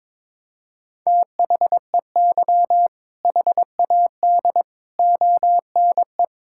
Morse code, heard as a keyed tone, about 22 wpm, 700 Hz. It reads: THEY HAD ONE
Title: Sentences from Top 1600 Words 22wpm